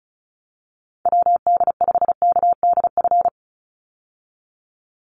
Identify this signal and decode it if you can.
Morse code, heard as a keyed tone, about 34 wpm, 700 Hz. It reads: WD5KDF
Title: USA Call Signs 35wpm